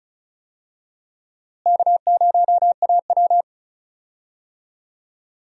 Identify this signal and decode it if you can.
Morse code, heard as a keyed tone, about 34 wpm, 700 Hz. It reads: K0AW
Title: USA Call Signs - 4 Characters - Repeated 3x - 35wpm